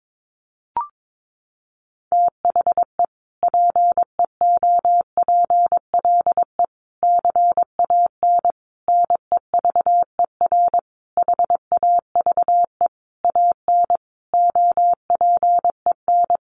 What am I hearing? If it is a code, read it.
Morse code, heard as a keyed tone, about 22 wpm, 700 Hz. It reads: THE PEOPLE CAN NEVER HAVE AN OPEN
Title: Sentences from Top 1400 Words 22wpm